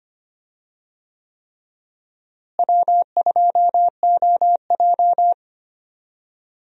Morse code, heard as a keyed tone, about 25 wpm, 700 Hz. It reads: W2OJ
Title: USA Call Signs - 4 Characters - Repeated 3x - 25wpm